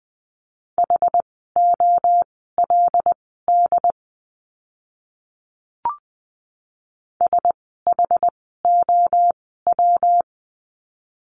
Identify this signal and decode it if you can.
Morse code, heard as a keyed tone, about 20 wpm, spaced as for 14 wpm, 700 Hz. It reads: HOLD SHOW
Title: 115 Lesson 19f - Letter w - Words 20wpm